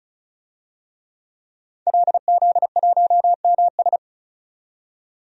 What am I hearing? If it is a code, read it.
Morse code, heard as a keyed tone, about 34 wpm, 700 Hz. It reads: LZ1MS
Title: DXCC-100 Call Signs - 5 Characters - 35wpm